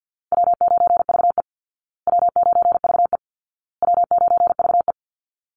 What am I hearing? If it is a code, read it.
Morse code, heard as a keyed tone, about 50 wpm, 700 Hz. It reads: W9VE W9VE W9VE
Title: Call Signs Repeated 2x - 50wpm